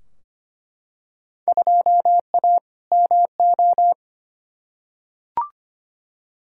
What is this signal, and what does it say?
Morse code, heard as a keyed tone, about 25 wpm, 700 Hz. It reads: 2A MO
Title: Field Day Exchanges - 25wpm